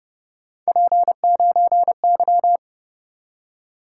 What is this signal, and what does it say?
Morse code, heard as a keyed tone, about 29 wpm, 700 Hz. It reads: P9Y
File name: Speed-Racing - Three Letter-Number Combinations - 25wpm